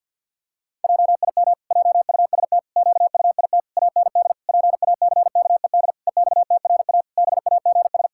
Morse code, heard as a keyed tone, about 51 wpm, 700 Hz. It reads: JIM JUST QUIT AND PACKED EXTRA BAGS
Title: Pangram Challenge v2 - 50wpm